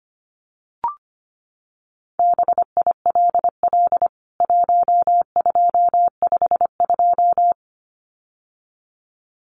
Morse code, heard as a keyed tone, about 25 wpm, 700 Hz. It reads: BILL 1252
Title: CWT Exchange 25wpm